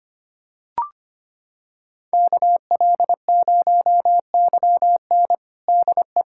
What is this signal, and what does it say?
Morse code, heard as a keyed tone, about 25 wpm, 700 Hz. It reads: KL0YN DE
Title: USA Callsign DE Callsign - 5 Characters - 25wpm